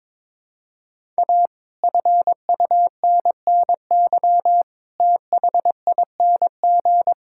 Morse code, heard as a keyed tone, about 22 wpm, 700 Hz. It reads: A FUNNY THING
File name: practice / 22wpm / Sentences with 8-Word Phrases - 22wpm